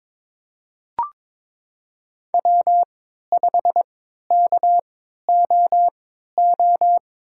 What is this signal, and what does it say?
Morse code, heard as a keyed tone, about 22 wpm, spaced as for 13 wpm, 700 Hz. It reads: W5KOO
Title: USA Call Signs - 5 Characters - 3x Character Spacing - 22wpm